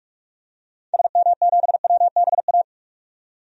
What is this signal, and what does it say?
Morse code, heard as a keyed tone, about 41 wpm, 700 Hz. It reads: SM7WBA
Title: DXCC-100 Call Signs - 6 Characters - 45wpm